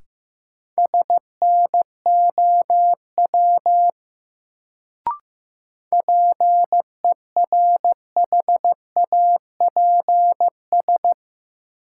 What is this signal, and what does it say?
Morse code, heard as a keyed tone, about 15 wpm, 700 Hz. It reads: SNOW PERHAPS
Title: Sentences from Top 500 Words - New Word Drill - 15wpm